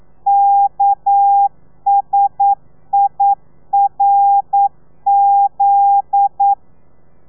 Morse code, heard as a keyed tone, about 9 wpm, 800 Hz. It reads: KSIRZ
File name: telegrafia / i2tfj / Lez9 40